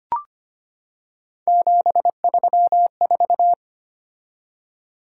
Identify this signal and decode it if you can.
Morse code, heard as a keyed tone, about 25 wpm, 700 Hz. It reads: 734
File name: Three Letter-Number Combinations - 25wpm